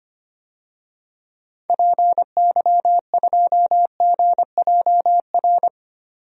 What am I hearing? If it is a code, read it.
Morse code, heard as a keyed tone, about 25 wpm, 700 Hz. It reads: PY2GJR
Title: DXCC-100 Call Signs - 6 Characters - 25wpm